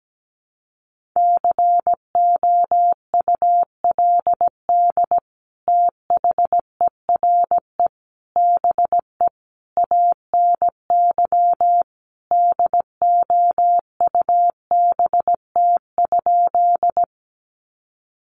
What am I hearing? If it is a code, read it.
Morse code, heard as a keyed tone, about 17 wpm, 700 Hz. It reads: COULD THERE BE ANY DOUBT?
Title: Sentences from Top 1500 Words 17wpm